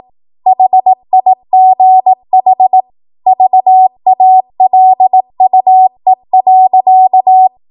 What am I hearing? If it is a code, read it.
Morse code, heard as a keyed tone, about 18 wpm, 750 Hz. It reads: HIGH VALUE.
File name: Archive / 18 WPM / 18WPM